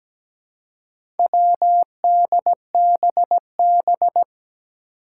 Morse code, heard as a keyed tone, about 17 wpm, 700 Hz. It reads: WDBB
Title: Four Challenging Letter-Number Combinations 17wpm